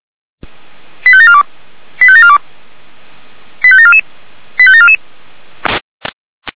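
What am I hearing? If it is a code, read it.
Five-tone selective call (ZVEI), heard as quick runs of five tones, tones of about 70 ms. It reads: 85742 85742 85744 85744
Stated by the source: ZVEI five-tone call 85385 received by SDR.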